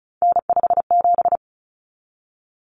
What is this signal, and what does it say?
Morse code, heard as a keyed tone, about 34 wpm, 700 Hz. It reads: N57
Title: Common Cut Numbers - 3-Digit 35wpm